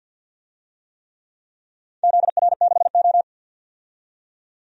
Morse code, heard as a keyed tone, about 49 wpm, 700 Hz. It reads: ZR6O
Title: DXCC-100 Call Signs - 4 Characters - 50wpm